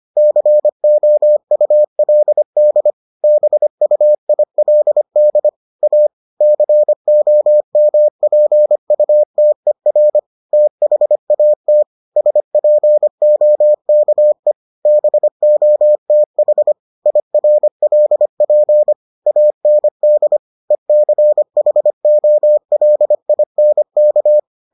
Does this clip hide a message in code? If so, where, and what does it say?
Yes: Morse code from 0.2 to 24.4 s, heard as a keyed tone, about 25 wpm, 600 Hz. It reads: COULD BUILD A COMPUTER THAT SPOKE BOTH IRLP AND ECHOLINK